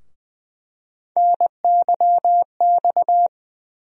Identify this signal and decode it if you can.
Morse code, heard as a keyed tone, about 21 wpm, 700 Hz. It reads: NYX
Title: USA Call Sign Postfixes - 3 Letters 20wpm